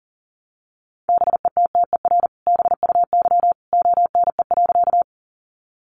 Morse code, heard as a keyed tone, about 40 wpm, 700 Hz. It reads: BETTER BUY ONE.
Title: Sentences from Top 700 Words 40wpm